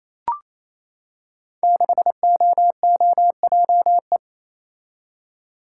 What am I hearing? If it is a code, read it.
Morse code, heard as a keyed tone, about 29 wpm, 700 Hz. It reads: 6OOJE
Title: Code Groups - 5 Letter:Numbers - 28wpm